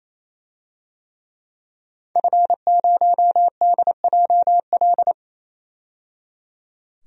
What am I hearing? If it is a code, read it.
Morse code, heard as a keyed tone, about 28 wpm, 700 Hz. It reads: F0DJL
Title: DXCC-100 Call Signs - 5 Characters - 28wpm